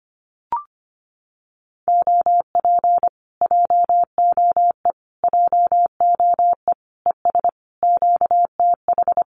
Morse code, heard as a keyed tone, about 25 wpm, 700 Hz. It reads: OP JOE JOE ES QTH
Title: Common QSO Phrases Encore 25wpm